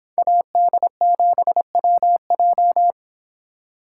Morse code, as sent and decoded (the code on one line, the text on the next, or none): .- -.. --... .-- .---
AD7WJ